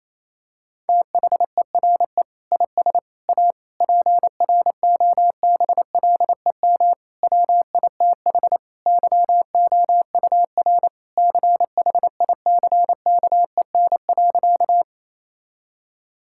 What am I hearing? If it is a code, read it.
Morse code, heard as a keyed tone, about 28 wpm, 700 Hz. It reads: THERE IS A PROBLEM WITH YOUR CHICKEN.
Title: Sentences from Top 1000 Words - Review Part 5 - 28wpm